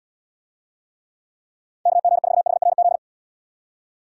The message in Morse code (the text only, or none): KG4HRZ